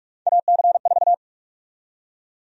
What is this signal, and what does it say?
Morse code, heard as a keyed tone, about 45 wpm, 700 Hz. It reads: AC4
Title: USA Call Sign Prefixes - 45wpm